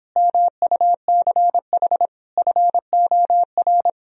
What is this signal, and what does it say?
Morse code, heard as a keyed tone, about 29 wpm, 700 Hz. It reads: MUCH FOR